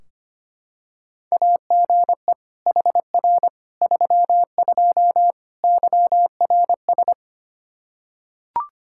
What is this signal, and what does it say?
Morse code, heard as a keyed tone, about 25 wpm, 700 Hz. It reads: AGE HR 32 YRS
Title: Practice QSO - Exchange 3 - More Info - 25wpm